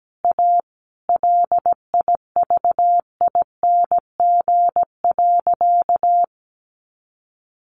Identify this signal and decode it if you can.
Morse code, heard as a keyed tone, about 17 wpm, 700 Hz. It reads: A LIVING.
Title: Sentences from Top 1500 Words 17wpm